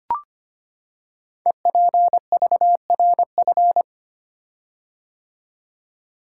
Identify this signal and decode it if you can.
Morse code, heard as a keyed tone, about 25 wpm, 700 Hz. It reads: EPVRF
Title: Code Groups - 5 Letters - 25wpm